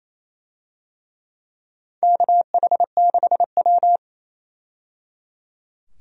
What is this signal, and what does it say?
Morse code, heard as a keyed tone, about 29 wpm, 690 Hz. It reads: KH6W
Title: DXCC-100 Call Signs - 4 Characters - 28wpm